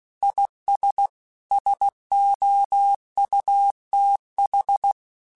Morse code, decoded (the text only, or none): IS SOUTH